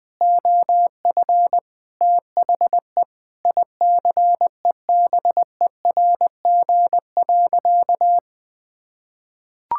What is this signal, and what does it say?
Morse code, heard as a keyed tone, about 21 wpm, 700 Hz. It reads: OF THE ICEBERG.
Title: Sentences with 7-Word Phrases - Spoken Beginning - 20wpm